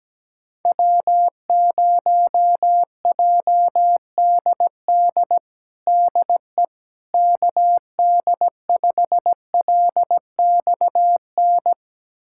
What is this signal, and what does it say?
Morse code, heard as a keyed tone, about 17 wpm, 700 Hz. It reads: W0JDD DE KD5LXN
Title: USA Callsign DE Callsign 17wpm